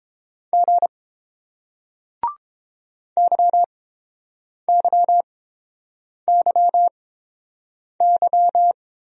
Morse code, heard as a keyed tone, about 29 wpm, spaced as for 7 wpm, 700 Hz. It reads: G YYYY